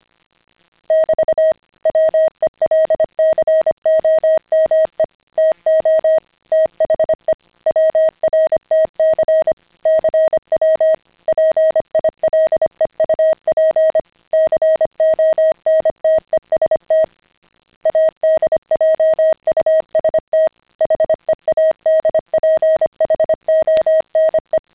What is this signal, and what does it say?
Morse code, heard as a keyed tone, about 25 wpm, 620 Hz. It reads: <BT> WELCOME TO THE WRTC CW PILEUP CONTEST ADJUST HEADPHONE
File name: CW-pile up